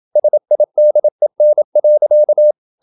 Morse code, heard as a keyed tone, about 27 wpm, 600 Hz. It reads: SIDEN.
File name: Ferie QRQ 27wpm